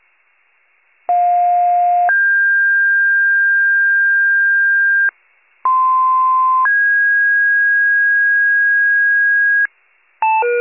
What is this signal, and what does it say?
Two-tone page, heard as a one-second tone, then a three-second tone, about 710 then 1600 Hz, about 1040 then 1650 Hz. It is heard from 1.1 to 9.7 s.